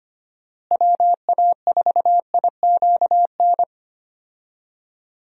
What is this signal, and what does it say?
Morse code, heard as a keyed tone, about 25 wpm, 700 Hz. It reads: WA4IQN
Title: USA Call Signs - 6 Characters - Repeated 3x - 25wpm